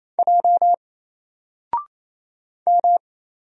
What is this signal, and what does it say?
Morse code, heard as a keyed tone, about 27 wpm, 700 Hz. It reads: J M
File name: Single-Letter-Number-28wpm